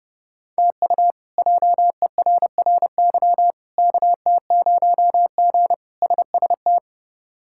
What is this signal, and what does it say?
Morse code, heard as a keyed tone, about 32 wpm, 700 Hz. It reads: TU JERRY KT0G SST
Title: K1USN SST QSO Practice - 30wpm